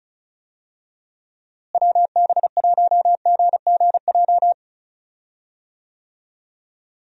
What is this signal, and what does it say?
Morse code, heard as a keyed tone, about 36 wpm, 700 Hz. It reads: WB1GGJ
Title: USA Call Signs - 6 Characters - Repeated 2x - 35wpm